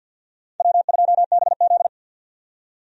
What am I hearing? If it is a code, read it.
Morse code, heard as a keyed tone, about 53 wpm, 700 Hz. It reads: W2BZ
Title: USA Call Signs - 4 Characters - 50wpm